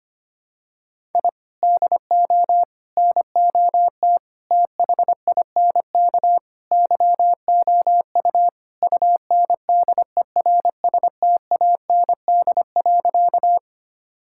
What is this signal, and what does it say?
Morse code, heard as a keyed tone, about 25 wpm, 700 Hz. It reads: I DO NOT THINK YOU UNDERSTAND.
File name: practice / 25wpm / Sentences with 4-Word Phrases - 25wpm